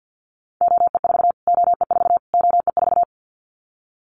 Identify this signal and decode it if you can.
Morse code, heard as a keyed tone, about 50 wpm, 700 Hz. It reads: OE4 OE4 OE4
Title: Callsign Prefixes for Top 25 DXCC - Repeated 3x - 50wpm